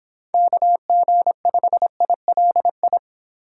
Morse code, heard as a keyed tone, about 25 wpm, 700 Hz. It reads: KG5ILI